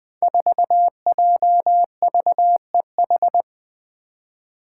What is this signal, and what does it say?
Morse code, heard as a keyed tone, about 21 wpm, 700 Hz. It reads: 4JVEH
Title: Five Challenging Letter-Number Combinations 20wpm